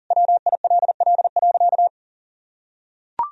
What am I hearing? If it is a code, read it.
Morse code, heard as a keyed tone, about 40 wpm, 700 Hz. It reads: WILL.
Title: Sentences from Top 500 Words and Abbreviations 40wpm